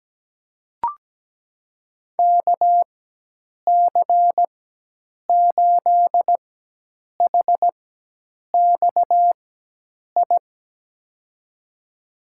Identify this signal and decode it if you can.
Morse code, heard as a keyed tone, about 17 wpm, spaced as for 8 wpm, 700 Hz. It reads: KC8HXI
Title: USA Call Signs - 6 Characters - 4x Character Spacing - 17wpm